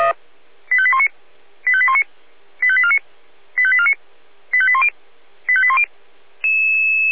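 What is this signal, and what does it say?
Five-tone selective call (ZVEI), heard as quick runs of five tones, tones of about 80 ms. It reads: 85819 85819 85849 85849 85810 85810
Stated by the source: ZVEI five-tone selective call 85870 received with an SDR.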